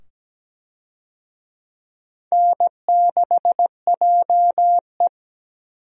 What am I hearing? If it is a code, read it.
Morse code, heard as a keyed tone, about 17 wpm, 700 Hz. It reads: N6JE
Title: USA Call Signs - 4 Characters - 17wpm